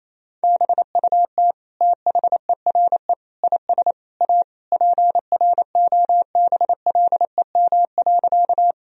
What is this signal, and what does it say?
Morse code, heard as a keyed tone, about 28 wpm, 700 Hz. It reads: BUT THERE IS A PROBLEM.
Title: Sentences from Top 1000 Words - Review Part 2 - 28wpm